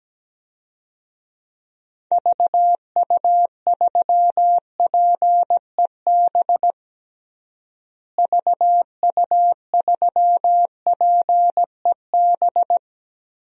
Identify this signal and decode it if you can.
Morse code, heard as a keyed tone, about 17 wpm, 700 Hz. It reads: VU3PEB VU3PEB
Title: DXCC-100 Call Signs - 6 Characters - Repeated 3x - 17wpm